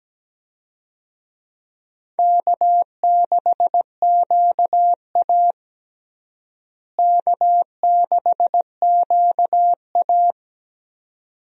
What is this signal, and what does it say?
Morse code, heard as a keyed tone, about 17 wpm, 700 Hz. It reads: K6QA K6QA
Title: USA Call Signs - 4 Characters - Repeated 2x - 17wpm